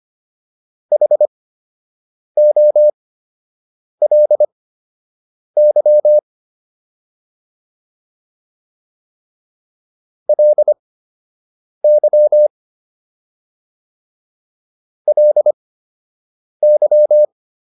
Morse code, heard as a keyed tone, about 25 wpm, spaced as for 7 wpm, 600 Hz. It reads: HOLY LY LY
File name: LY4 07